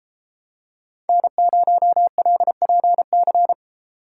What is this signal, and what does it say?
Morse code, heard as a keyed tone, about 31 wpm, 700 Hz. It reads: N0LPC